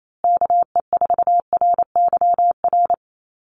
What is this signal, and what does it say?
Morse code, heard as a keyed tone, about 28 wpm, 700 Hz. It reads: KE4RYR
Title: USA Call Signs Repeated 3x 28wpm